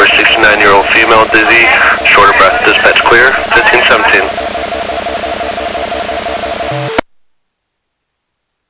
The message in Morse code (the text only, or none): none